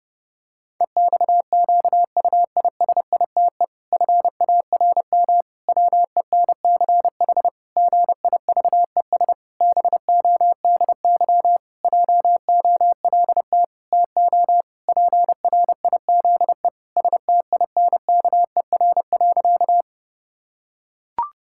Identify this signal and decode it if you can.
Morse code, heard as a keyed tone, about 30 wpm, 700 Hz. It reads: EXQUISITE FARM WENCH GIVES BODY JOLT TO PRIZE STINKER.
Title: Pangram Challenge v2 - 30wpm